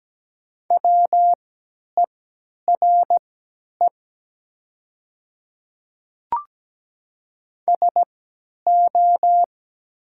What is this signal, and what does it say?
Morse code, heard as a keyed tone, about 17 wpm, spaced as for 10 wpm, 700 Hz. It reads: WERE SO
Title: 3x Character Spacing - Top 100 Words - 17wpm